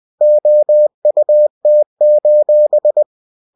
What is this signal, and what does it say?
Morse code, heard as a keyed tone, about 20 wpm, 600 Hz. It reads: OUT:
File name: cw-books / winnie-the-pooh / winnie-20wpm-page-0013